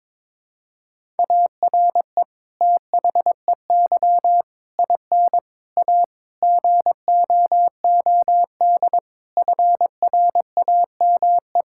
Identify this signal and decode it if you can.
Morse code, heard as a keyed tone, about 22 wpm, 700 Hz. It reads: ARE THEY IN A GOOD FRAME
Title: Sentences from Top 1000 Words - Review Part 6 - 22wpm